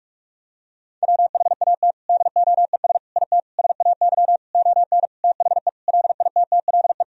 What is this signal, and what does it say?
Morse code, heard as a keyed tone, about 46 wpm, 700 Hz. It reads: WHAT DOES IT SAY ON THE LITTLE
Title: Sentences from Top 1000 Words - Review Part 2 - 45wpm